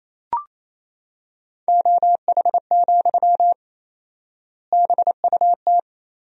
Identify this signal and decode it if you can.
Morse code, heard as a keyed tone, about 28 wpm, 700 Hz. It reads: OH, BUT
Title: 4x Word-Spacing - Sentences from Top 500 Words - 28wpm